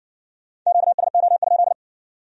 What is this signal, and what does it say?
Morse code, heard as a keyed tone, about 55 wpm, 700 Hz. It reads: QSO?